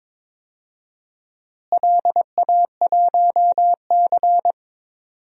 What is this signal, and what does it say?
Morse code, heard as a keyed tone, about 22 wpm, 700 Hz. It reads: LA1C
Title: DXCC-100 Call Signs - 4 Characters - Repeated 3x - 22wpm